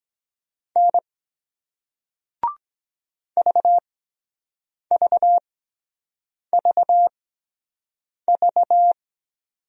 Morse code, heard as a keyed tone, about 23 wpm, spaced as for 7 wpm, 700 Hz. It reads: N VVVV